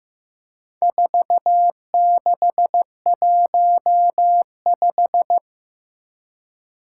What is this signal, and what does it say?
Morse code, heard as a keyed tone, about 15 wpm, 700 Hz. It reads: 4615
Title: Four Challenging Letter-Number Combinations 15wpm